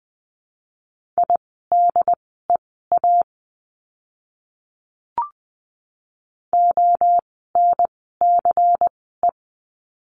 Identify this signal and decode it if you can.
Morse code, heard as a keyed tone, about 20 wpm, spaced as for 15 wpm, 700 Hz. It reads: IDEA ONCE
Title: 108 Lesson 18f - Letter m - Words 20wpm